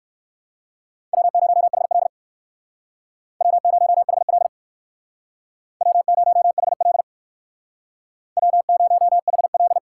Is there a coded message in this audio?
Yes: Morse code from 1.1 to 9.8 s, heard as a keyed tone, about 59 wpm, 700 Hz. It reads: W0HL W0HL W0HL W0HL